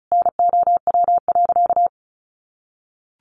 Morse code, heard as a keyed tone, about 34 wpm, 700 Hz. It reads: NOW.
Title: Sentences from Top 1300 Words 35wpm